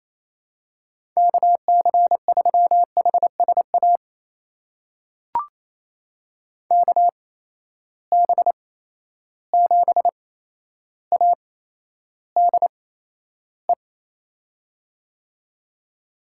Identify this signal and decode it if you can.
Morse code, heard as a keyed tone, about 28 wpm, spaced as for 12 wpm, 700 Hz. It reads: KC3HSA KB7ADE
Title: USA Call Signs - 6 Characters - 8x Character Spacing - 28wpm